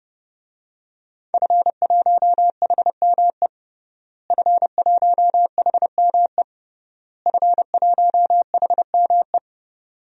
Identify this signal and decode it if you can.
Morse code, heard as a keyed tone, about 30 wpm, 700 Hz. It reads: F1HME F1HME F1HME
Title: DXCC-100 Call Signs - 5 Characters - Repeated 3x - 30wpm